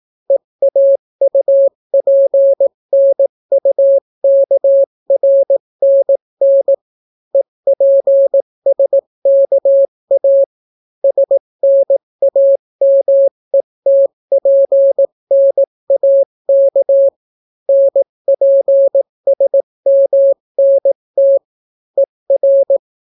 Morse code, heard as a keyed tone, about 18 wpm, 550 Hz. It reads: EAUPNUKRNN EPSKA SNAMETPNAK NPSMNT ER